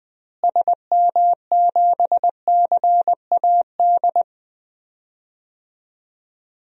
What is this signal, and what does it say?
Morse code, heard as a keyed tone, about 20 wpm, 690 Hz. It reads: SM7CAD
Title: DXCC-100 Call Signs - 6 Characters - Repeated 2x - 20wpm